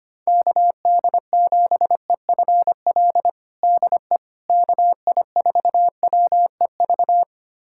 Morse code, heard as a keyed tone, about 25 wpm, 700 Hz. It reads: KD7EFL DE KI4WEV
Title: USA Callsign DE Callsign - 6 Characters - 25wpm